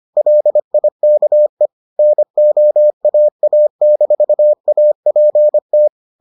Morse code, heard as a keyed tone, about 25 wpm, 600 Hz. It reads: LIKE NOAA-APT